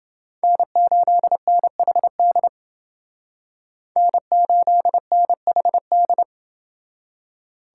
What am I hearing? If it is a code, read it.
Morse code, heard as a keyed tone, about 29 wpm, 700 Hz. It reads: N8NHD N8NHD